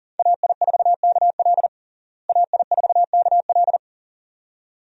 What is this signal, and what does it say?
Morse code, heard as a keyed tone, about 37 wpm, 700 Hz. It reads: AI4KL AI4KL
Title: USA Call Signs - 5 Characters - Repeated 2x - 40wpm